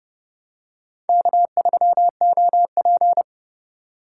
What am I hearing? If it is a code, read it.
Morse code, heard as a keyed tone, about 28 wpm, 700 Hz. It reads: K3OP